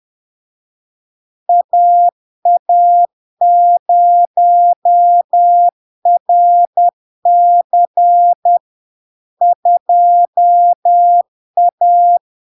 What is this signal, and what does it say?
Morse code, heard as a keyed tone, about 10 wpm, 700 Hz. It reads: AA0RC 2A
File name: LARK-FD-2022-Log-in-CW-10wpm-700-Hz